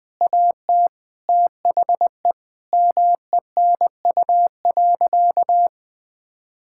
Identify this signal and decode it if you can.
Morse code, heard as a keyed tone, about 20 wpm, 700 Hz. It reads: AT THE MENU.
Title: Sentences with 4-Word Phrases - 20wpm